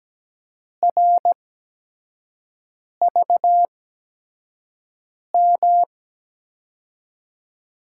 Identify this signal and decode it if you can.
Morse code, heard as a keyed tone, about 17 wpm, spaced as for 5 wpm, 700 Hz. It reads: RVM
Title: USA Call Signs - 5 Characters - 8x Character Spacing - 17wpm